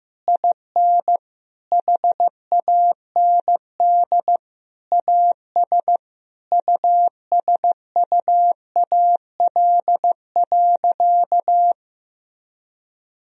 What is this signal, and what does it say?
Morse code, heard as a keyed tone, about 15 wpm, 700 Hz. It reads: IN HAND AS USUAL.
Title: Sentences from Top 1000 Words - Review Part 2 - 15wpm